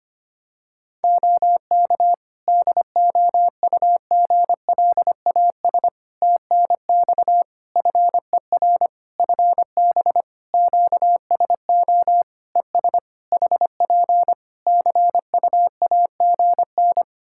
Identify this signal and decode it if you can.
Morse code, heard as a keyed tone, about 25 wpm, 700 Hz. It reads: OK DOUGLAS TNX FER FB QSO ES HP CUAGN
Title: Practice QSO - Exchange 4 - Ending - 25wpm